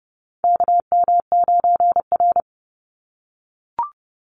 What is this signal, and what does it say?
Morse code, heard as a keyed tone, about 30 wpm, 700 Hz. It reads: KM9R
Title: Call Signs - 30wpm